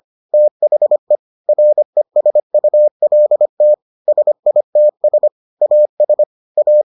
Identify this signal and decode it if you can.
Morse code, heard as a keyed tone, about 25 wpm, 600 Hz. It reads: THE RESULT SITS AS A